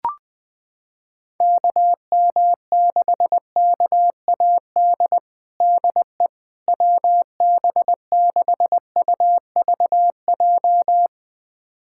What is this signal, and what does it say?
Morse code, heard as a keyed tone, about 20 wpm, 700 Hz. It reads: KM6KAD DE WB6UVJ
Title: USA Callsign DE Callsign - 6 Characters - 20wpm